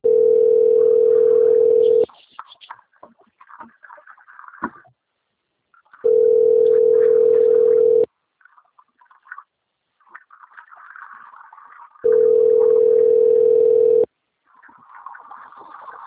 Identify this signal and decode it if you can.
Ringback tone (North America), 440 + 480 Hz.
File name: out- - - - - .44052